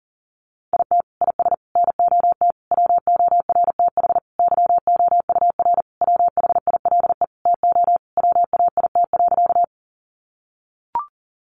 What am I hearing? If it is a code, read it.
Morse code, heard as a keyed tone, about 40 wpm, 700 Hz. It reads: IT IS NOT WORTH YOUR WHILE TO WAIT.
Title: Sentences from Top 800 Words 40wpm